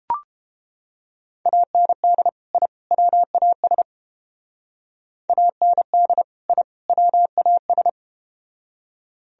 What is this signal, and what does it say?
Morse code, heard as a keyed tone, about 33 wpm, 700 Hz. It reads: AND I WAS AND I WAS